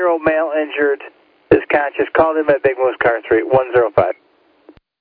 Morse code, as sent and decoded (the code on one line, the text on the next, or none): none